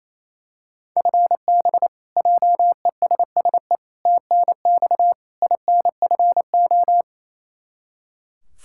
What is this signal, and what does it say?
Morse code, heard as a keyed tone, about 28 wpm, 700 Hz. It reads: FB JESSE TNX INFO
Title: K1USN SST QSO Practice 28wpm